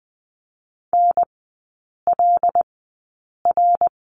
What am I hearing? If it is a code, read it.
Morse code, heard as a keyed tone, about 20 wpm, spaced as for 8 wpm, 700 Hz. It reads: NLR
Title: 074 Lesson 13e - Letter L - Three Characters 20wpm